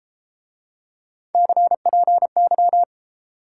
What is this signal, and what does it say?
Morse code, heard as a keyed tone, about 34 wpm, 700 Hz. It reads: CPY